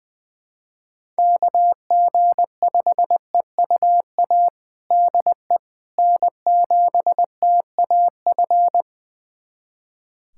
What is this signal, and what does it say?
Morse code, heard as a keyed tone, about 20 wpm, 700 Hz. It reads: KG5EUA DE N7TAF
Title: USA Callsign DE Callsign 20wpm